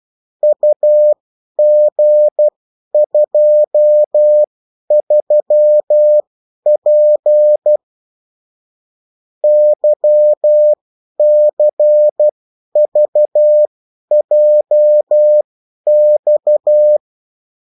Morse code, heard as a keyed tone, about 12 wpm, spaced as for 10 wpm, 600 Hz. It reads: UG23P YCVJX
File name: morse mp3 / revision9-vitesse 10 mots